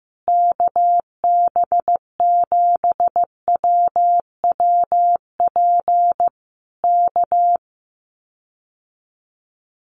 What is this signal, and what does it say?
Morse code, heard as a keyed tone, about 15 wpm, 700 Hz. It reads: KB7WWP K
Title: Common QSO Phrases Encore 15wpm